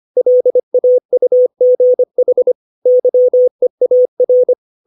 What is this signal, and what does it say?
Morse code, heard as a keyed tone, about 25 wpm, 500 Hz. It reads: LAUGH YEAR